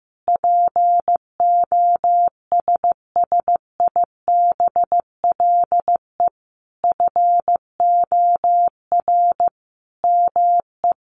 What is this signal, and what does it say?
Morse code, heard as a keyed tone, about 15 wpm, 700 Hz. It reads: POSSIBLE FOR ME
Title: Sentences from Top 500 Words 15wpm